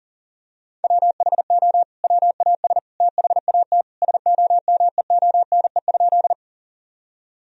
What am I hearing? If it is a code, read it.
Morse code, heard as a keyed tone, about 40 wpm, 700 Hz. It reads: WHO WAS THAT SOMEONE?
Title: Sentences from Top 1000 Words - Review Part 4 - 40wpm